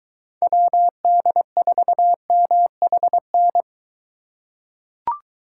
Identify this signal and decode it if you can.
Morse code, heard as a keyed tone, about 23 wpm, 700 Hz. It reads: WD4MHN